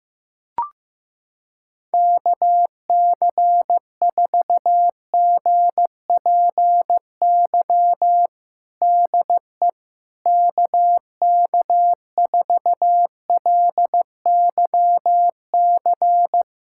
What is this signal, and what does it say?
Morse code, heard as a keyed tone, about 15 wpm, 700 Hz. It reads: KC4GPY DE KK4LYC
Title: USA Callsign DE Callsign - 6 Characters - 15wpm